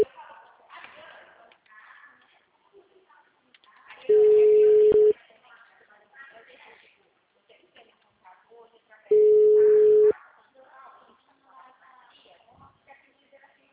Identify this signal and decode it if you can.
Ringback tone (Europe, 425 Hz), with speech.
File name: out- -2050- - - .57596